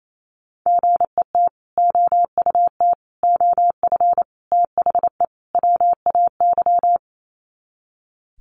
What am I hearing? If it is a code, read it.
Morse code, heard as a keyed tone, about 28 wpm, 700 Hz. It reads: GET OUT OF THE WAY
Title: Sets of 5 Words Encore 28wpm